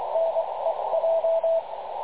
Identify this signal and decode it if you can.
Morse code, heard as a keyed tone, about 22 wpm, spaced as for 19 wpm, 660 Hz, with radio noise. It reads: NEJ
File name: dlazaawansowanych / nagraniazpasma / websdr recording 3567.0kHz